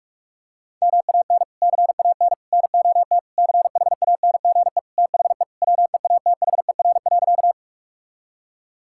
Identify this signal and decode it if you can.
Morse code, heard as a keyed tone, about 41 wpm, 700 Hz. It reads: MAN CAN NOT CHANGE THE WEATHER.
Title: Sentences from Top 1000 Words - Review Part 5 - 45wpm